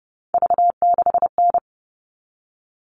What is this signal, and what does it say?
Morse code, heard as a keyed tone, about 30 wpm, 700 Hz. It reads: V6N
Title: All Cut Numbers - 3-Digit 30wpm